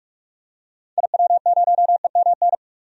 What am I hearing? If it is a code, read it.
Morse code, heard as a keyed tone, about 46 wpm, 700 Hz. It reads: IW0EMN
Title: DXCC-100 Call Signs - 6 Characters - 45wpm